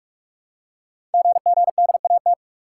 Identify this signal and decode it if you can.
Morse code, heard as a keyed tone, about 44 wpm, 690 Hz. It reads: GGDAT
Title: Code Groups - 5 Letters - 45wpm